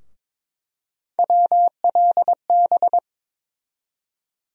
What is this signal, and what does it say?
Morse code, heard as a keyed tone, about 22 wpm, 700 Hz. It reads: WLB
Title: USA Call Sign Postfixes - 3 Letters 22wpm